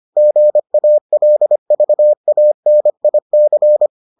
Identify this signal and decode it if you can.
Morse code, heard as a keyed tone, about 25 wpm, 600 Hz. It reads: GALVANIC